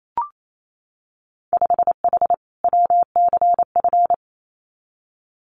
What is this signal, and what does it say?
Morse code, heard as a keyed tone, about 28 wpm, 700 Hz. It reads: 5H WCF
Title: Winter Field Day Exchanges Encore 28wpm